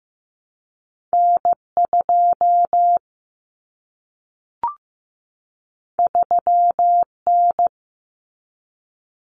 Morse code, heard as a keyed tone, about 15 wpm, 700 Hz. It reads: N2 3N
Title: Common Cut Numbers - 2-Digit 15wpm